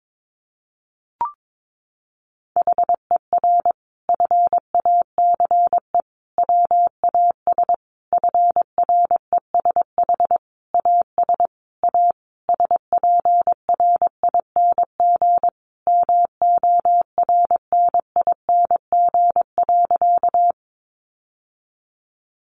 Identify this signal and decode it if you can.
Morse code, heard as a keyed tone, about 22 wpm, 700 Hz. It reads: HER FACE WAS FRESH AS A SPRING MORNING.
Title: Sentences from Top 1200 Words 22wpm